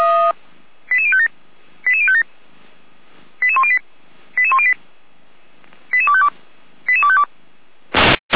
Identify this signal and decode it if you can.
Five-tone selective call (ZVEI), heard as quick runs of five tones, tones of about 70 ms. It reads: 88057 88057 88108 88108 88252 88252